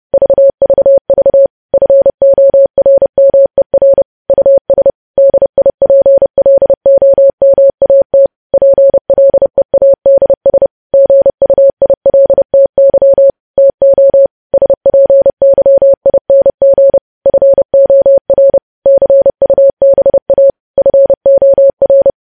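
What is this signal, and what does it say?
Morse code, heard as a keyed tone, about 30 wpm, 550 Hz. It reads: VVV FORMER US DIPLOMAT PLEADS GUILTY TO SPYING FOR CUBA FOR
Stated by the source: Morse code sending of news headlines at 30 wpm.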